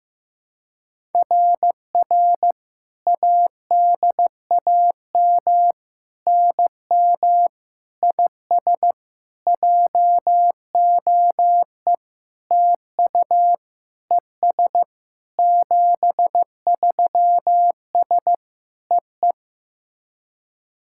Morse code, heard as a keyed tone, about 15 wpm, 700 Hz. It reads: RR ADAM NM IS JOE TU ES 73S EE
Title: SOTA QSO Practice 15wpm